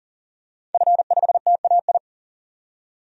Morse code, heard as a keyed tone, about 40 wpm, 700 Hz. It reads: F5TAI
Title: DXCC-100 Call Signs - 5 Characters - 40wpm